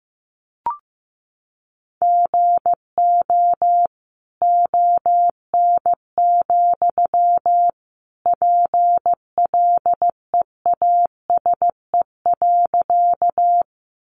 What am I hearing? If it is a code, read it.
Morse code, heard as a keyed tone, about 15 wpm, 700 Hz. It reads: GO ON, PLEASE.
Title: Sentences from Top 1500 Words 15wpm